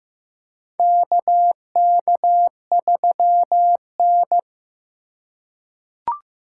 Morse code, heard as a keyed tone, about 15 wpm, 700 Hz. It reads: KK3N
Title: USA Call Signs - 4 Characters - 2x Character Spacing - 15wpm